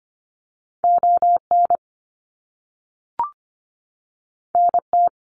Morse code, heard as a keyed tone, about 25 wpm, 700 Hz. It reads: ON NT
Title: Canadian Provinces and Territories 25wpm